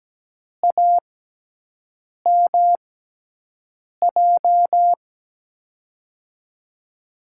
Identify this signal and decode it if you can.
Morse code, heard as a keyed tone, about 17 wpm, spaced as for 6 wpm, 700 Hz. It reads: AMJ
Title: USA Call Signs - 6 Characters - 6x Character Spacing - 17wpm